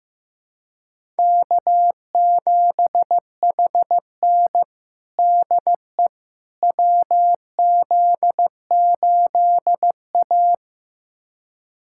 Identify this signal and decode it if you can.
Morse code, heard as a keyed tone, about 15 wpm, 700 Hz. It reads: K7HN DE WZ8A
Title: USA Callsign DE Callsign - 4 Characters - 15wpm